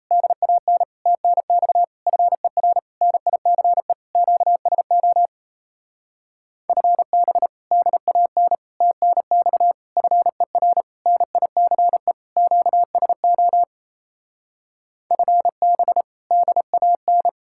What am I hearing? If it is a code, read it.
Morse code, heard as a keyed tone, about 38 wpm, 700 Hz. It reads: DAN TNX FER NICE QSO FB DAN TNX FER NICE QSO FB DAN